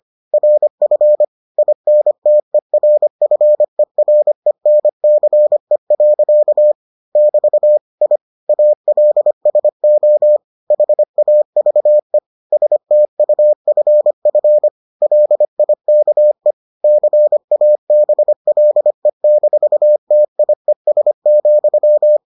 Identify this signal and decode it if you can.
Morse code, heard as a keyed tone, about 25 wpm, 600 Hz. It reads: RF INTERFERENCE. <BT> I ALSO HAVE STUFF LIKE CABLE-TIES,